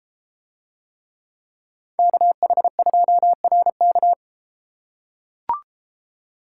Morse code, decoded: KH2RK

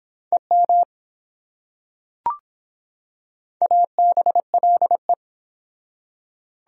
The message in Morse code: EM ABLE

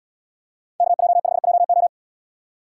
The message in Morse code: KJ5JO

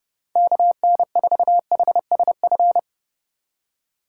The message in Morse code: KN4HSF

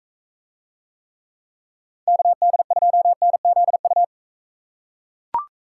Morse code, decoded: KD2NZU

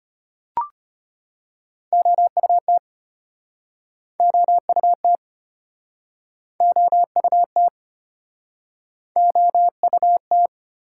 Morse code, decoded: OUT OUT OUT OUT

